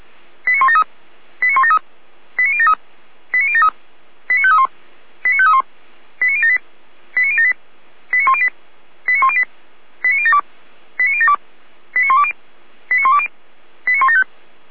Five-tone selective call (ZVEI), heard as quick runs of five tones, tones of about 70 ms. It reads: 89173 89173 89073 89073 89531 89531 89078 89078 89108 89108 89072 89072 89120 89120 89175
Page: SDR/ZVEI